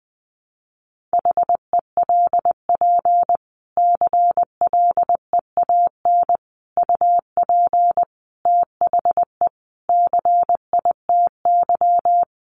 Morse code, heard as a keyed tone, about 20 wpm, 700 Hz. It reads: HELP CLEAN UP THE CITY
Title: Sentences from Top 1100 Words 20wpm